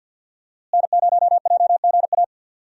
Morse code, heard as a keyed tone, about 45 wpm, 700 Hz. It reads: N0JGA
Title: USA Call Signs - 5 Characters - Repeated 3x - 50wpm